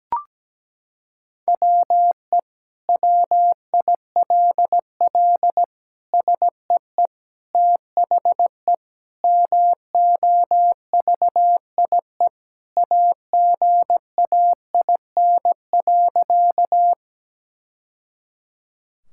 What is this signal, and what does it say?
Morse code, heard as a keyed tone, about 17 wpm, 700 Hz. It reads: WE WILL SEE THE MOVIE AGAIN.
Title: Sentences from Top 1000 Words - Review Part 5 - 17wpm